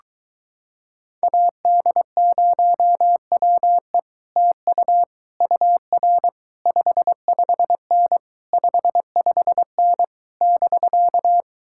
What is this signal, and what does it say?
Morse code, heard as a keyed tone, about 22 wpm, 700 Hz. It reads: AD0WE TU UR 55N 55N <BK>